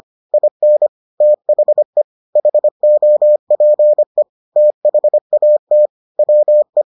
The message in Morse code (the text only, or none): IN THE HOPE THAT WE